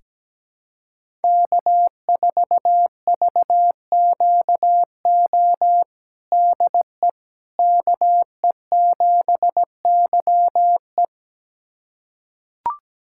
Morse code, heard as a keyed tone, about 17 wpm, 700 Hz. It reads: K4VQO DE KE7YE
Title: USA Callsign DE Callsign - 5 Characters - 17wpm